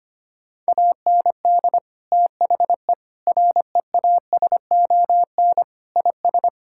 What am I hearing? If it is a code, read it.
Morse code, heard as a keyed tone, about 25 wpm, 700 Hz. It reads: AND THE REASON IS
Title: Sentences from Top 1600 Words 25wpm